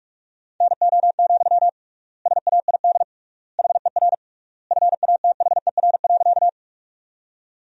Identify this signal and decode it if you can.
Morse code, heard as a keyed tone, about 46 wpm, 700 Hz. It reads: NO, SAID HER FATHER.
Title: 3x Word-Spacing - Sentences from Top 300 Words - 45wpm